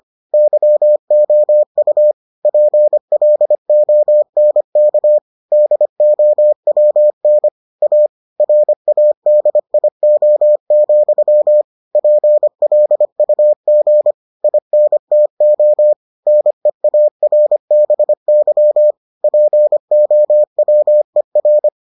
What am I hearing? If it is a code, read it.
Morse code, heard as a keyed tone, about 25 wpm, 600 Hz. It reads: YOU PLONK DOWN A RADIO, PLUG INTO NEARBY POWER